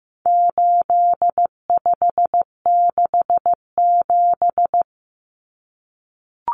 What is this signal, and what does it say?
Morse code, heard as a keyed tone, about 15 wpm, 700 Hz. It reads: 8567
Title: Numbers - 4-Digit 15wpm